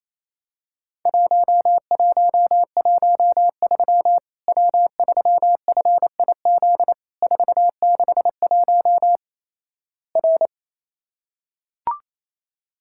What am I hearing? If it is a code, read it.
Morse code, heard as a keyed tone, about 28 wpm, 700 Hz. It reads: 1113 W3FIZ 461 R
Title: Worked All Europe DX Contest - QTC Exchange - 28wpm